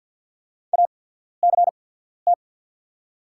Morse code, heard as a keyed tone, about 48 wpm, spaced as for 13 wpm, 690 Hz. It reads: ACT
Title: 8x Character Spacing - Top 200 Words - 50wpm